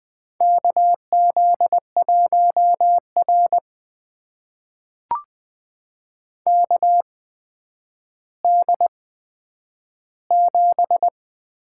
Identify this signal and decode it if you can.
Morse code, heard as a keyed tone, about 21 wpm, spaced as for 10 wpm, 700 Hz. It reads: KZ1R KD7
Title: USA Call Signs - 8x Character Spacing - 20wpm